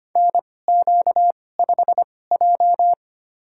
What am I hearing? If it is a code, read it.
Morse code, heard as a keyed tone, about 25 wpm, spaced as for 18 wpm, 700 Hz. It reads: NQ5J
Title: USA Call Signs - 2x Character Spacing - 25wpm